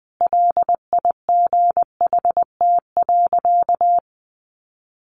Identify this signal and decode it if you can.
Morse code, heard as a keyed tone, about 20 wpm, 700 Hz. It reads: LIGHT.
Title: A Sound Of Thunder 20wpm